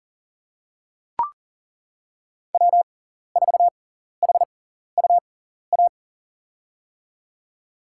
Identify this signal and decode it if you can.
Morse code, heard as a keyed tone, about 40 wpm, spaced as for 14 wpm, 700 Hz. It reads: W4HUA
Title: USA Call Signs - 5 Characters - 6x Character Spacing - 40wpm